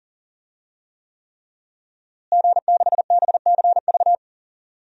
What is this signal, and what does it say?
Morse code, heard as a keyed tone, about 38 wpm, 700 Hz. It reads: G6BCV
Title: DXCC-100 Call Signs - 5 Characters - Repeated 3x - 40wpm